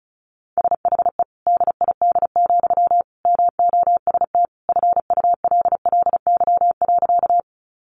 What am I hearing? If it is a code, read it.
Morse code, heard as a keyed tone, about 34 wpm, 700 Hz. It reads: SHE DID, MOST FULLY.
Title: Sentences from Top 1300 Words 35wpm